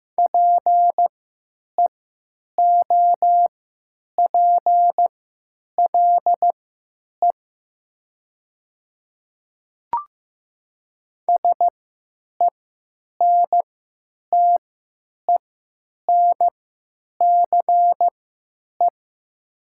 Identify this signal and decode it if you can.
Morse code, heard as a keyed tone, about 15 wpm, spaced as for 9 wpm, 700 Hz. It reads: PEOPLE SENTENCE
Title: 3x Character Spacing - Top 500 Words - 15wpm